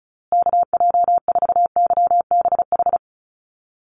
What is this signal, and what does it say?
Morse code, heard as a keyed tone, about 34 wpm, 700 Hz. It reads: KJ4YBH
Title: USA Call Signs Repeated 2x 35wpm